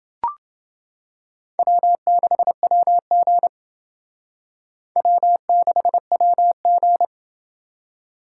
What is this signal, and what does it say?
Morse code, heard as a keyed tone, about 30 wpm, 700 Hz. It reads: W6WG W6WG